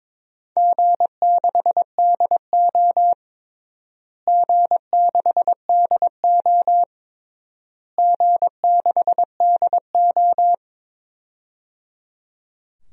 Morse code, heard as a keyed tone, about 22 wpm, 700 Hz. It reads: G6DO G6DO G6DO
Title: DXCC-100 Call Signs - 4 Characters - Repeated 3x - 22wpm